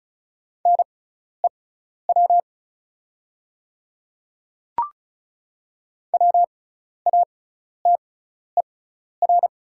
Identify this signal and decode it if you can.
Morse code, heard as a keyed tone, about 36 wpm, spaced as for 12 wpm, 690 Hz. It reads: NEW WATER
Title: 6x Character Spacing - Top 200 Words - 35wpm